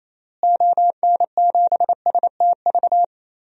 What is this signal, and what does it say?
Morse code, heard as a keyed tone, about 28 wpm, 700 Hz. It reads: ON7STV
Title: DXCC-100 Call Signs - 6 Characters - Repeated 2x - 28wpm